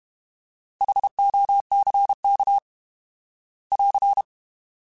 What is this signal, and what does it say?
Morse code, heard as a keyed tone, about 32 wpm, 800 Hz. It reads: HOCK <AR>